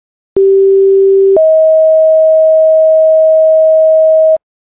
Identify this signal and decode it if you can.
Two-tone page, heard as a one-second tone, then a three-second tone, about 390 then 640 Hz, with noise. It is heard from 0.4 to 4.4 s.